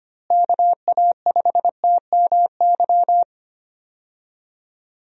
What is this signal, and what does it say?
Morse code, heard as a keyed tone, about 25 wpm, 700 Hz. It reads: KA5TMY
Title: USA Call Signs - 6 Characters - Repeated 3x - 25wpm